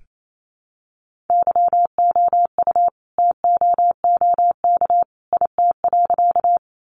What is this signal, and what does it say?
Morse code, heard as a keyed tone, about 28 wpm, 700 Hz. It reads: YOU TOOK IT.
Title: Sentences from Top 300 Words 28wpm